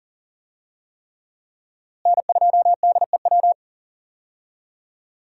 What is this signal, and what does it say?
Morse code, heard as a keyed tone, about 40 wpm, 700 Hz. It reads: N2DEW
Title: USA Call Signs - 5 Characters - Repeated 3x - 40wpm